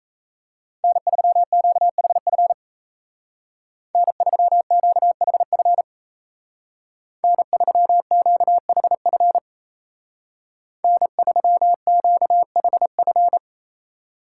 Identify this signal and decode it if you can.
Morse code, heard as a keyed tone, about 36 wpm, 700 Hz. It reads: N3QHF N3QHF N3QHF N3QHF